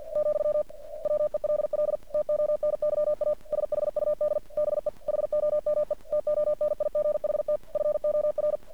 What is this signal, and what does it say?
Morse code, heard as a keyed tone, about 50 wpm, 600 Hz. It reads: , WELL TONYA SHUD BE HOME TONIGHT FOR
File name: archived / CRSnet